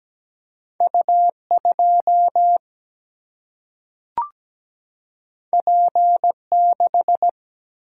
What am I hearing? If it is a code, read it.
Morse code, heard as a keyed tone, about 17 wpm, 700 Hz. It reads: U2 P6
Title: Single-Letter Single-Number 17wpm